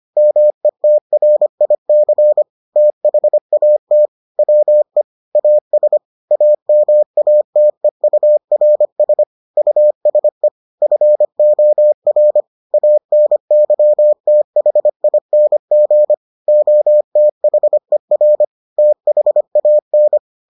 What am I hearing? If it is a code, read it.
Morse code, heard as a keyed tone, about 25 wpm, 600 Hz. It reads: METRIC THAT WE AS AMATEURS USE FOR ANYTHING OTHER THAN